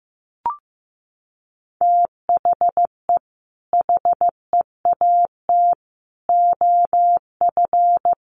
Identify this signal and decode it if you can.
Morse code, heard as a keyed tone, about 15 wpm, 700 Hz. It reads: THE HEAT OF
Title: Sentences from Top 1100 Words 15wpm